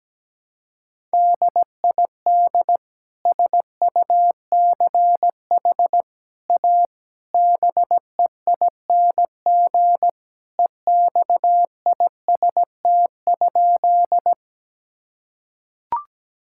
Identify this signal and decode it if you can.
Morse code, heard as a keyed tone, about 17 wpm, 700 Hz. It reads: DID SUCH A BEING EXIST?
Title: Sentences from Top 1000 Words - Review Part 5 - 17wpm